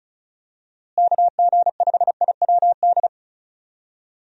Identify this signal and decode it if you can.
Morse code, heard as a keyed tone, about 36 wpm, 700 Hz. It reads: KG5IWD
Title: USA Call Signs - 6 Characters - Repeated 2x - 35wpm